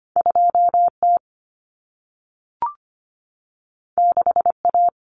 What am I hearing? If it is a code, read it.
Morse code, heard as a keyed tone, about 25 wpm, 700 Hz. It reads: 2T 6A
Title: Common Cut Numbers - 2-Digit 25wpm